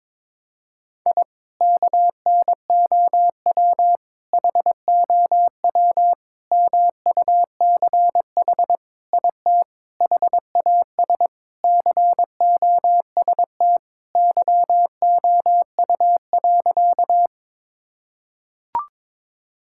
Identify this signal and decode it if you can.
Morse code, heard as a keyed tone, about 22 wpm, 700 Hz. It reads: I KNOW HOW MUCH IT HAS COST YOU.
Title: Sentences from Top 1000 Words - Review Part 5 - 22wpm